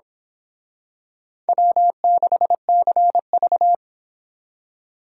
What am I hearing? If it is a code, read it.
Morse code, heard as a keyed tone, about 25 wpm, 700 Hz. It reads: W6CV
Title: Speed Racing - USA Call Signs 22wpm